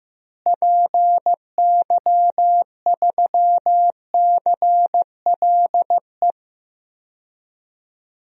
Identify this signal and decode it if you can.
Morse code, heard as a keyed tone, about 16 wpm, 700 Hz. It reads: PY3CLE
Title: DXCC-100 Call Signs - 6 Characters - Repeated 2x - 15wpm